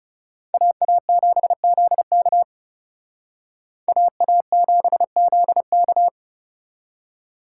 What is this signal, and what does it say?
Morse code, heard as a keyed tone, about 36 wpm, 700 Hz. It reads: AA7ZK AA7ZK
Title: Speed Racing - USA Call Signs 30wpm